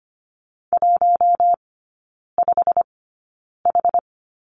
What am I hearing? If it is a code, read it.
Morse code, heard as a keyed tone, about 25 wpm, spaced as for 9 wpm, 700 Hz. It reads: 15H
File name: Lesson 15e - Number 5 - Three Characters 25wpm